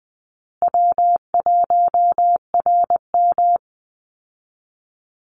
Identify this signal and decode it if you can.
Morse code, heard as a keyed tone, about 20 wpm, 700 Hz. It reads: W1RM
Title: Easy Call Signs Repeated 3x - 20wpm